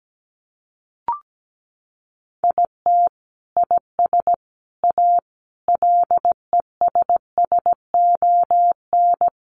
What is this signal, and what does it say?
Morse code, heard as a keyed tone, about 17 wpm, 700 Hz. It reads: IT IS A LESSON
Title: Sentences from Top 1200 Words 17wpm